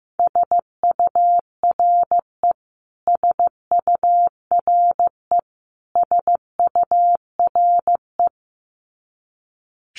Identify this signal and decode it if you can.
Morse code, heard as a keyed tone, about 15 wpm, 700 Hz. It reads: SURE SURE SURE
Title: Top 500 Words Repeated 15wpm